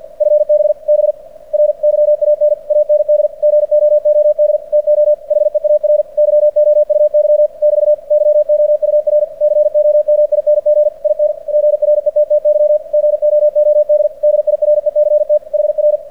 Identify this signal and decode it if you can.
Morse code, heard as a keyed tone, about 45 wpm, 600 Hz. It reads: FB BRYAN AND GOOD TO HEAR YOUY <BT> YOUR COMING IN GRETTY GOOD DIRECT HR
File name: archived / CRSnet